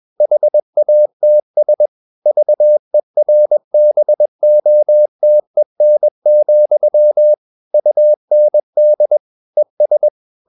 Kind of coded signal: Morse code